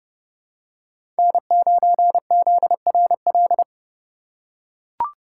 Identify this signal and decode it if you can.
Morse code, heard as a keyed tone, about 30 wpm, 690 Hz. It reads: N9ZRL
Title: USA Call Signs - 5 Characters - 2x Character Spacing - 30wpm